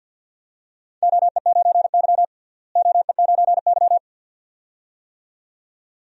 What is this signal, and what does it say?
Morse code, heard as a keyed tone, about 48 wpm, 700 Hz. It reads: OE9Y OE9Y
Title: DXCC-100 Call Signs - 4 Characters - Repeated 2x - 50wpm